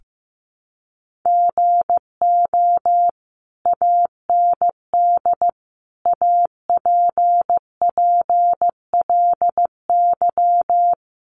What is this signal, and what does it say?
Morse code, heard as a keyed tone, about 15 wpm, 700 Hz. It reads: GO AND APPLY
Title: Sentences from Top 1200 Words 15wpm